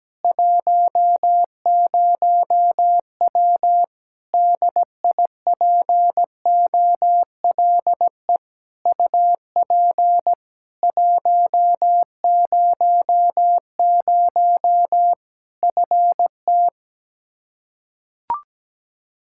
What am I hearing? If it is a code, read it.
Morse code, heard as a keyed tone, about 17 wpm, 700 Hz. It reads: 10W DIPOLE UP 100 FT
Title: Practice QSO - Exchange 2 - Info - 17wpm